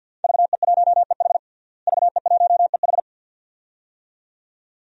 Morse code, heard as a keyed tone, about 50 wpm, 700 Hz. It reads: VE1EH VE1EH
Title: DXCC-100 Call Signs - 5 Characters - Repeated 2x - 50wpm